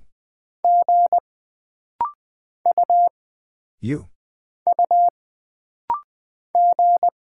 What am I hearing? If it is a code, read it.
Morse code, heard as a keyed tone, about 20 wpm, 700 Hz. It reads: G U U G
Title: Single Character (Letter:Number:Punctuation:Prosign) - Rapid-Fire - 20wpm